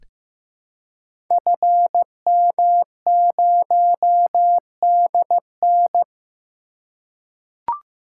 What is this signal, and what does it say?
Morse code, heard as a keyed tone, about 15 wpm, 700 Hz. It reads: FM0DN
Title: Code Groups - 5 Letter:Numbers - 15wpm